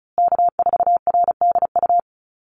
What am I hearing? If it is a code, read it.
Morse code, heard as a keyed tone, about 34 wpm, 700 Hz. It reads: K4RDU
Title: USA Call Signs Repeated 3x 35wpm